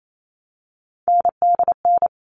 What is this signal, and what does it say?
Morse code, heard as a keyed tone, about 28 wpm, 700 Hz. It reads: NDN
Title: All Cut Numbers - 3-Digit 28wpm